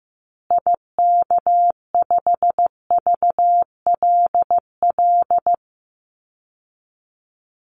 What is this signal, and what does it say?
Morse code, heard as a keyed tone, about 15 wpm, 700 Hz. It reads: IK5VLL
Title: Call Signs Repeated 2x - 15wpm